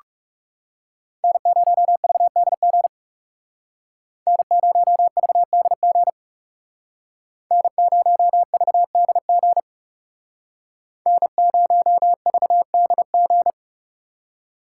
Morse code, heard as a keyed tone, about 37 wpm, 700 Hz. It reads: N0VDG N0VDG N0VDG N0VDG
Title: Speed Racing - USA 1x3 Call Signs 30wpm